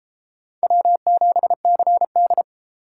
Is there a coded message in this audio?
Yes: Morse code from 0.6 to 2.4 s, heard as a keyed tone, about 33 wpm, 700 Hz. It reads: W7CD